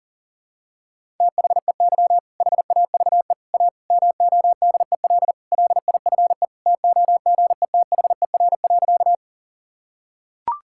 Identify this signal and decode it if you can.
Morse code, heard as a keyed tone, about 40 wpm, 700 Hz. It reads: THEY HAVE A MODEL LIFE TOGETHER.
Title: Sentences from Top 1000 Words - Review Part 5 - 40wpm